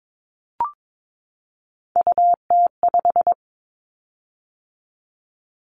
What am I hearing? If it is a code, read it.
Morse code, heard as a keyed tone, about 22 wpm, 700 Hz. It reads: UT5
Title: Callsign Prefixes for Top 25 DXCC - 22wpm